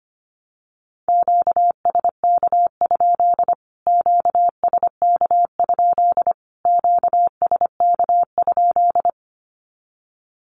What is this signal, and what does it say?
Morse code, heard as a keyed tone, about 25 wpm, 700 Hz. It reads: QSK? QSK? QSK?
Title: Q-Codes Repeated 3x 25wpm